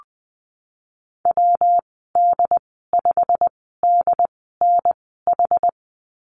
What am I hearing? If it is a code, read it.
Morse code, heard as a keyed tone, about 20 wpm, spaced as for 14 wpm, 700 Hz. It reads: WD5DNH
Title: 173 Lesson 28g - Number 9 - Call Signs 20wpm